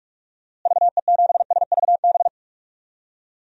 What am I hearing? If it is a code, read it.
Morse code, heard as a keyed tone, about 41 wpm, 700 Hz. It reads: VE7SVB
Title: DXCC-100 Call Signs - 6 Characters - Repeated 2x - 45wpm